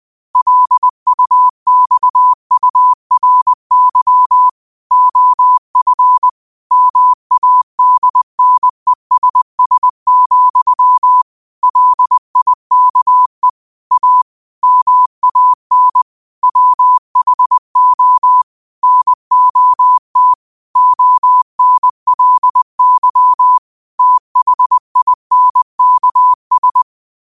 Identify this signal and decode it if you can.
Morse code, heard as a keyed tone, about 20 wpm, 1000 Hz. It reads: LUXURY OF MADNESS, LIKE A MAN WHO NOT ONLY THINKS
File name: qotdfiles / 20wpm